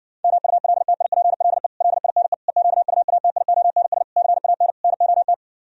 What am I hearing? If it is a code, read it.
Morse code, heard as a keyed tone, about 59 wpm, 700 Hz. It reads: MULTIPLE LINE EQUATIONS CAN NOT